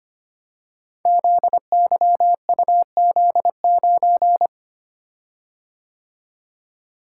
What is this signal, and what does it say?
Morse code, heard as a keyed tone, about 25 wpm, 700 Hz. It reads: ZYUZ9
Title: Code Groups - 5 Letter:Numbers - 25wpm